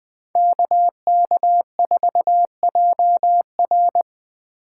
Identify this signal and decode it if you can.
Morse code, heard as a keyed tone, about 23 wpm, 700 Hz. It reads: KK4JR
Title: Speed Racing - USA 2x2 Call Signs 20wpm